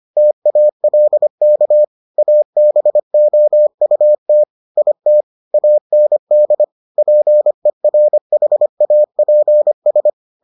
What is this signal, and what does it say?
Morse code, heard as a keyed tone, about 25 wpm, 600 Hz. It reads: TALK ABOUT IT AND PERHAPS